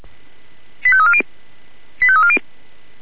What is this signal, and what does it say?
Five-tone selective call (ZVEI), heard as quick runs of five tones, tones of about 70 ms. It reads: 85340 85340
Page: /496/SDR/ZVEI/85340/